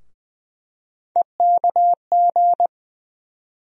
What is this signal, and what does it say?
Morse code, heard as a keyed tone, about 20 wpm, 700 Hz. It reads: EKG
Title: USA Call Sign Postfixes - 3 Letters 20wpm